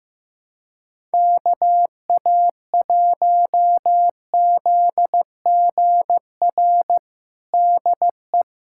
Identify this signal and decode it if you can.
Morse code, heard as a keyed tone, about 16 wpm, 700 Hz. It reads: KA1ZGR DE
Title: USA Callsign DE Callsign - 6 Characters - 15wpm